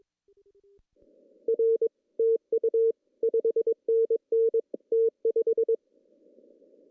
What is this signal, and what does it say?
Morse code, heard as a keyed tone, about 21 wpm, 450 Hz. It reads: R TU 5NN T5
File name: CW / cqww-cw-k5tr